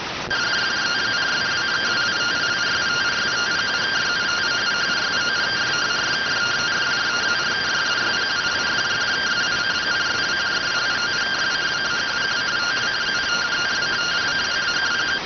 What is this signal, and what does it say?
Two-tone teleprinter signal, about 170 Hz shift.